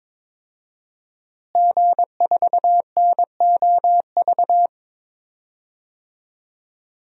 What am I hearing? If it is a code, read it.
Morse code, heard as a keyed tone, about 22 wpm, 700 Hz. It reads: G4NOV
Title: DXCC-100 Call Signs - 5 Characters - Repeated 2x - 22wpm